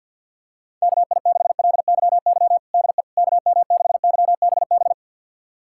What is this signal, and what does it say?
Morse code, heard as a keyed tone, about 50 wpm, 700 Hz. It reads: KI6LYY DE KM6YBB
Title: USA Callsign DE Callsign - 6 Characters - 50wpm